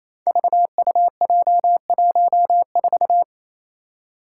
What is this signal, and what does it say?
Morse code, heard as a keyed tone, about 28 wpm, 700 Hz. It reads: VUJ14
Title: Five Challenging Letter-Number Combinations 28wpm